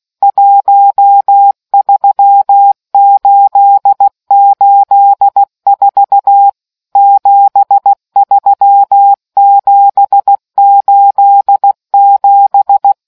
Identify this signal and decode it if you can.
Morse code, heard as a keyed tone, about 16 wpm, 800 Hz. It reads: 13884 73787